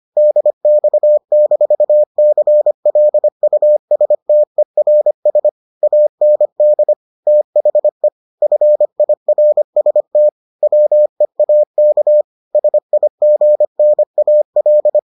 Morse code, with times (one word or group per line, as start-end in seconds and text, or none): 0.2-5.5 DX-CLUSTERS
5.8-6.9 AND
7.3-8.1 THE
8.4-10.3 FIRST
10.6-12.2 WEAK
12.5-15.0 SIGNAL